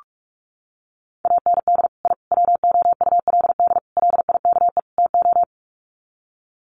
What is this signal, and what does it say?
Morse code, heard as a keyed tone, about 44 wpm, 700 Hz. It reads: AND I WOULD LIKE TO
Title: Sets of 5 Words Encore 45wpm